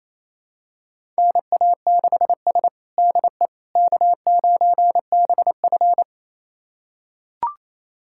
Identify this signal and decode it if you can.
Morse code, heard as a keyed tone, about 28 wpm, 700 Hz. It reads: NA6S DE K9BF
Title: USA Callsign DE Callsign - 4 Characters - 28wpm